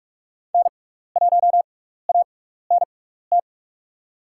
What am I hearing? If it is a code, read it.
Morse code, heard as a keyed tone, about 47 wpm, spaced as for 16 wpm, 700 Hz. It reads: N1ANT
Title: USA Call Signs - 6x Character Spacing - 45wpm